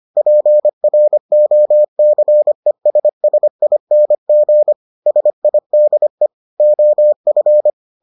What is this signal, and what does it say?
Morse code, heard as a keyed tone, about 25 wpm, 600 Hz. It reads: PROCESSING SIDE OF